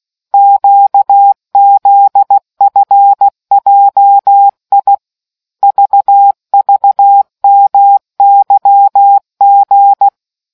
Morse code, heard as a keyed tone, about 16 wpm, 800 Hz. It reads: QZFJI VVMYG